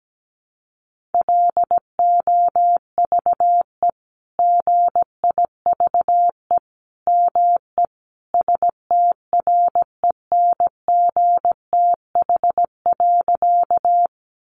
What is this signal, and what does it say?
Morse code, heard as a keyed tone, about 17 wpm, 700 Hz. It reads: LOVE GIVE ME STRENGTH.
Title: Sentences from Top 1300 Words 17wpm